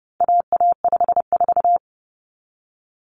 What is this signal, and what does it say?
Morse code, heard as a keyed tone, about 30 wpm, 700 Hz. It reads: AA54
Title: Common Cut Numbers - 4-Digit 30wpm